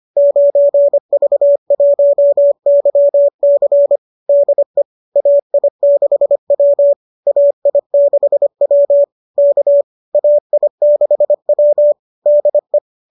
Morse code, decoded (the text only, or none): 9V1YC DE AI6W AI6W K AI6W DE